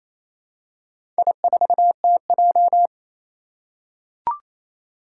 Morse code, heard as a keyed tone, about 27 wpm, 700 Hz. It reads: I4TJ
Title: Four Challenging Letter-Number Combinations 28wpm